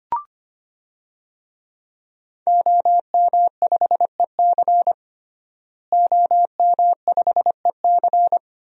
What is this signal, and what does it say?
Morse code, heard as a keyed tone, about 25 wpm, 700 Hz. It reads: OM5EC OM5EC
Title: DXCC-100 Call Signs - 5 Characters - Repeated 3x - 25wpm